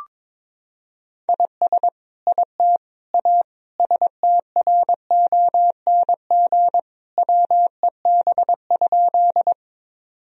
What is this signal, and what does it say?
Morse code, heard as a keyed tone, about 21 wpm, 700 Hz. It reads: IS IT A STRONG WEB?
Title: Sentences from Top 1000 Words - Review Part 4 - 22wpm